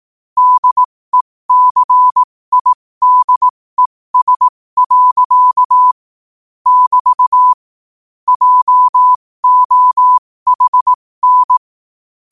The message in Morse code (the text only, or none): DECIDES. <BT> JOHN